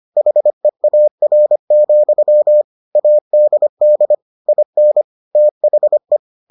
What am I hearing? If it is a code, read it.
Morse code, heard as a keyed tone, about 25 wpm, 600 Hz. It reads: HEAR, ADD IN THE